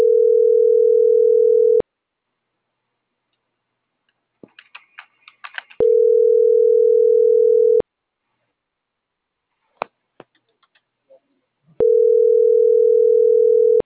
Ringback tone (North America), 440 + 480 Hz.
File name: CENTRAL MABE / out saby.reyes25